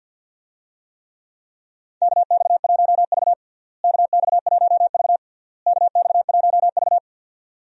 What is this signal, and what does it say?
Morse code, heard as a keyed tone, about 50 wpm, 700 Hz. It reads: KX1V KX1V KX1V
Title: USA Call Signs - 4 Characters - Repeated 3x - 50wpm